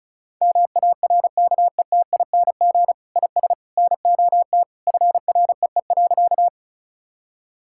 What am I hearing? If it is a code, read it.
Morse code, heard as a keyed tone, about 36 wpm, 700 Hz. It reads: MARKETING IS NOT FREE.
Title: Sentences from Top 1000 Words - Review Part 5 - 35wpm